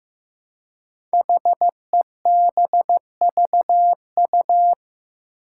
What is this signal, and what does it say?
Morse code, heard as a keyed tone, about 15 wpm, 700 Hz. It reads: HEBVU
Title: Five Challenging Letter-Number Combinations 15wpm